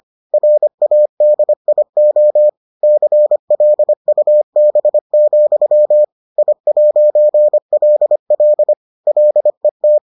Morse code, heard as a keyed tone, about 25 wpm, 600 Hz. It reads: RADIO CLUB, I'LL LET